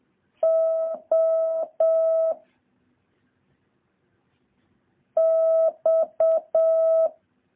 Morse code, heard as a keyed tone, about 7 wpm, 650 Hz. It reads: O X
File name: jh8cbh / 2023cw